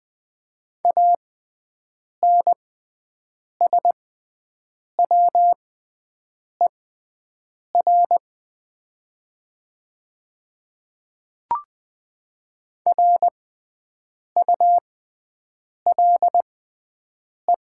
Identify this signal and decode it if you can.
Morse code, heard as a keyed tone, about 20 wpm, spaced as for 7 wpm, 700 Hz. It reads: ANSWER RULE
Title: 6x Character Spacing - Top 500 Words - 20wpm